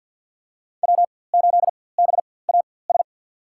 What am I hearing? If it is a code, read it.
Morse code, heard as a keyed tone, about 49 wpm, spaced as for 23 wpm, 700 Hz. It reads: W8BAS
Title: USA Call Signs - 5 Characters - 4x Character Spacing - 50wpm